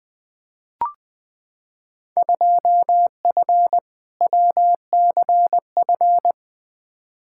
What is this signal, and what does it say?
Morse code, heard as a keyed tone, about 20 wpm, 700 Hz. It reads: 2F WCF
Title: Field Day Exchanges Encore - 20wpm